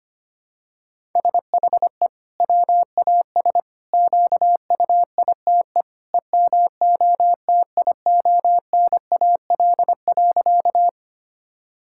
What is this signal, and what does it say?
Morse code, heard as a keyed tone, about 25 wpm, 700 Hz. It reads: SHE WAS QUITE EMOTIONAL.
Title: Sentences from Top 1000 Words - Review Part 5 - 25wpm